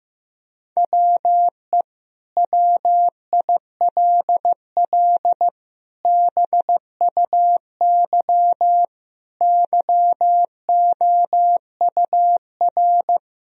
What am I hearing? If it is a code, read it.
Morse code, heard as a keyed tone, about 15 wpm, 700 Hz. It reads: WE WILL BUY YOUR
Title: Sentences from Top 1000 Words - Review Part 4 - 15wpm